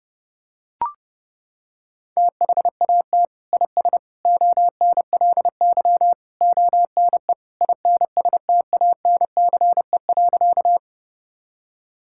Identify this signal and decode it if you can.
Morse code, heard as a keyed tone, about 30 wpm, 700 Hz. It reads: THAT IS ONLY ONE INSTANCE.
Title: Sentences from Top 1000 Words - Review Part 5 - 30wpm